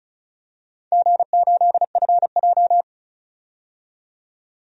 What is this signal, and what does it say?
Morse code, heard as a keyed tone, about 36 wpm, 700 Hz. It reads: G8FJ
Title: DXCC-100 Call Signs - 4 Characters - 35wpm